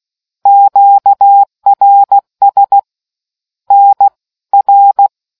Morse code, heard as a keyed tone, about 16 wpm, 800 Hz. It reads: QRS NR